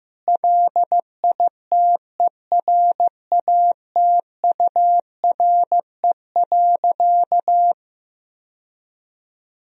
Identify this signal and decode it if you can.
Morse code, heard as a keyed tone, about 15 wpm, 700 Hz. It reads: LITERATURE.
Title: Sentences from Top 1000 Words - Review Part 3 - 15wpm